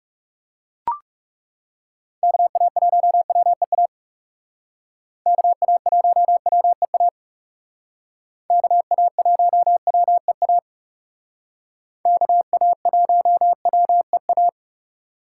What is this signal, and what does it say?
Morse code, heard as a keyed tone, about 38 wpm, 700 Hz. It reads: KA1WEA KA1WEA KA1WEA KA1WEA